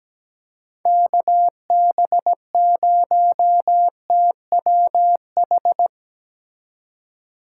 Morse code, heard as a keyed tone, about 20 wpm, 700 Hz. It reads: KB0TWH
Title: Speed Racing - USA Call Signs 17wpm